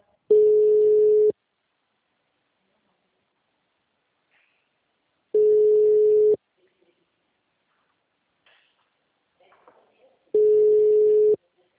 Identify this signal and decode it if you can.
Ringback tone (Europe, 425 Hz).
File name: out- -2016- - - .10116